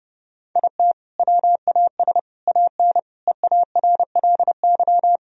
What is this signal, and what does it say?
Morse code, heard as a keyed tone, about 31 wpm, 700 Hz. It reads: IT WAS AN EARLY
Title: Sentences from Top 1000 Words - Review Part 4 - 30wpm